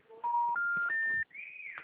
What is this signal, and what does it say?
Special information tone, with noise.
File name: out- -2010- - - .105